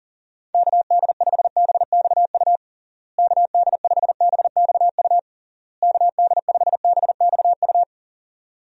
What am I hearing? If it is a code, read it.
Morse code, heard as a keyed tone, about 40 wpm, 700 Hz. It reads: KD5BXU KD5BXU KD5BXU
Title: USA Call Signs - 6 Characters - Repeated 3x - 40wpm